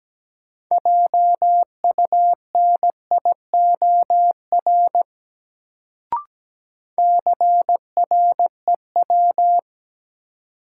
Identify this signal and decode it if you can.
Morse code, heard as a keyed tone, about 17 wpm, 700 Hz. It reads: JUNIOR CREW
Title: Sentences from Top 1500 Words - New Word Drill Alt - 17wpm